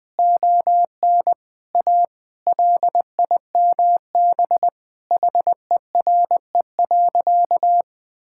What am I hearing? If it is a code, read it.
Morse code, heard as a keyed tone, about 21 wpm, 700 Hz. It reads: ON A LIMB HERE.
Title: Sentences with 8-Word Phrases - 20wpm